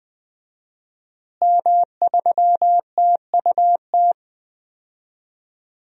Morse code, heard as a keyed tone, about 20 wpm, 700 Hz. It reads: M3TUT
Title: DXCC-100 Call Signs - 5 Characters - Repeated 2x - 20wpm